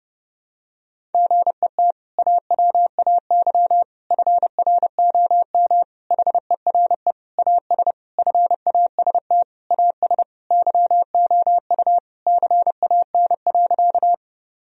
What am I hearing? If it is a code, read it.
Morse code, heard as a keyed tone, about 30 wpm, 700 Hz. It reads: GET AWAY FROM HERE AS FAST AS YOU CAN.
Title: Sentences from Top 1000 Words - Review Part 2 - 30wpm